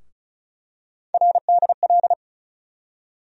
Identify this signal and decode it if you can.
Morse code, heard as a keyed tone, about 34 wpm, 700 Hz. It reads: RDL
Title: USA Call Sign Postfixes - 3 Letters 35wpm